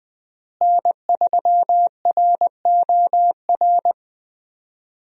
Morse code, heard as a keyed tone, about 21 wpm, 700 Hz. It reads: N3ROR
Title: USA Call Signs - 5 Characters - Repeated 2x - 20wpm